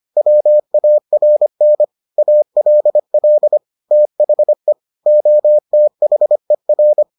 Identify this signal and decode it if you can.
Morse code, heard as a keyed tone, about 25 wpm, 600 Hz. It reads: WARN ALL THE OTHER